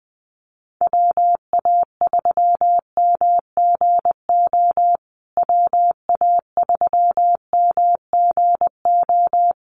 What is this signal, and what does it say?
Morse code, heard as a keyed tone, about 19 wpm, 700 Hz. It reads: WA3MGO WA3MGO
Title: USA Call Signs Repeated 3x 20wpm